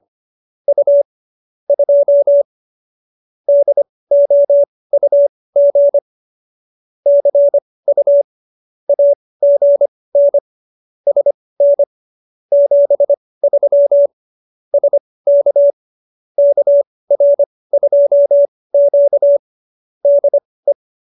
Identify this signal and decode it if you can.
Morse code, heard as a keyed tone, about 25 wpm, spaced as for 18 wpm, 570 Hz. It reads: U 2 DOUG CU AGN SN 73 SK KR2Q DE